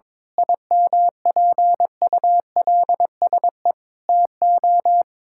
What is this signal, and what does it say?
Morse code, heard as a keyed tone, about 22 wpm, 700 Hz. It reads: IMPULSE TO
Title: Pangram Challenge v2 - 22wpm